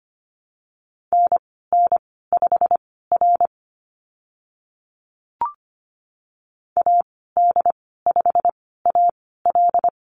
Morse code, heard as a keyed tone, about 25 wpm, spaced as for 16 wpm, 690 Hz. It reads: NN5R AD5AL
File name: Lesson 15g - Number 5 - Call Signs 25wpm